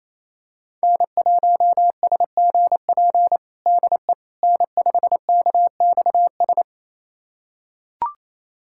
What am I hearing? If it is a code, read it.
Morse code, heard as a keyed tone, about 28 wpm, 700 Hz. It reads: N1SGP DE N5KXS
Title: USA Callsign DE Callsign - 5 Characters - 28wpm